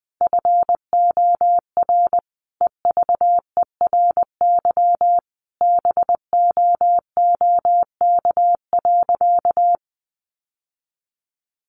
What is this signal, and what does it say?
Morse code, heard as a keyed tone, about 20 wpm, 700 Hz. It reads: FOR EVERY BOOK.
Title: Sentences from Top 1500 Words 20wpm